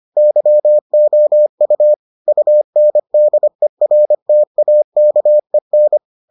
Morse code, heard as a keyed tone, about 25 wpm, 600 Hz. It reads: YOU UNDERTAKEN